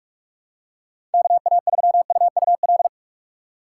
Morse code, heard as a keyed tone, about 42 wpm, 700 Hz. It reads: KA3UUL
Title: USA Call Signs - 6 Characters - 2x Character Spacing - 45wpm